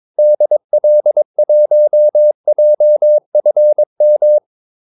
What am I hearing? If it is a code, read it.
Morse code, heard as a keyed tone, about 22 wpm, 600 Hz. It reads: DL1JFM
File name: Callsigns-22wpm